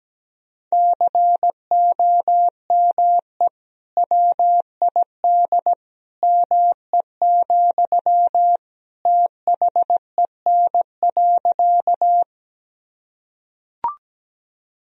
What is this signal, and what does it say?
Morse code, heard as a keyed tone, about 17 wpm, 700 Hz. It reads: COME WID ME, THEN.
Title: Sentences from Top 200 Words and Abbreviations 17wpm